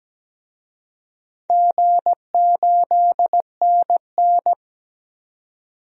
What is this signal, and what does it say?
Morse code, heard as a keyed tone, about 17 wpm, 700 Hz. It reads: G8NN
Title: DXCC-100 Call Signs - 4 Characters - 17wpm